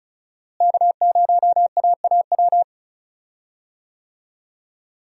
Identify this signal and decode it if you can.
Morse code, heard as a keyed tone, about 36 wpm, 700 Hz. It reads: K0AAW
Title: USA Call Signs - 5 Characters - Repeated 2x - 35wpm